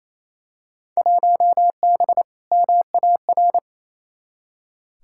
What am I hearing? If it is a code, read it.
Morse code, heard as a keyed tone, about 28 wpm, 700 Hz. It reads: 1B MAR
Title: Field Day Exchanges Encore - 28wpm